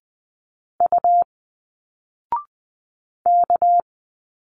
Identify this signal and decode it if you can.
Morse code, heard as a keyed tone, about 20 wpm, 700 Hz. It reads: U K